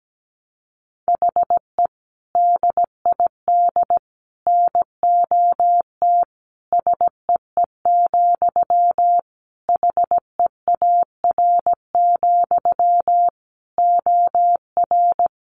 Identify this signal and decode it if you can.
Morse code, heard as a keyed tone, about 17 wpm, 700 Hz. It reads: HE DID NOT SEE, HEAR, OR
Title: Sentences from Top 1300 Words 17wpm